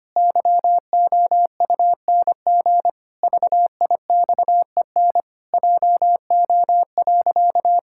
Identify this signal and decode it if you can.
Morse code, heard as a keyed tone, about 25 wpm, 700 Hz. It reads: YOUNG VIXEN JO.
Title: Pangram Challenge v2 - 25wpm